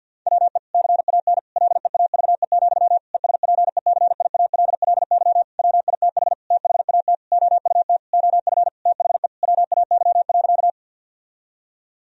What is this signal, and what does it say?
Morse code, heard as a keyed tone, about 50 wpm, 700 Hz. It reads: WE CAN LEAVE, ESPECIALLY WITH THAT OUT OF THE WAY.
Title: Sentences from Top 1000 Words - Review Part 4 - 50wpm